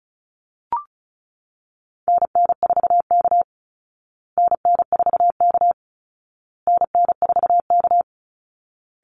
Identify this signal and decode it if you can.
Morse code, heard as a keyed tone, about 34 wpm, 700 Hz. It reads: NN4K NN4K NN4K
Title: Call Signs Repeated 2x - 35wpm